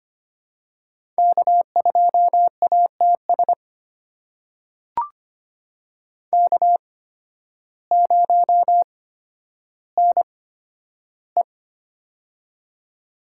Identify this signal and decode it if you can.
Morse code, heard as a keyed tone, about 25 wpm, spaced as for 12 wpm, 700 Hz. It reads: K2ATS K0NE
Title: USA Call Signs - 8x Character Spacing - 25wpm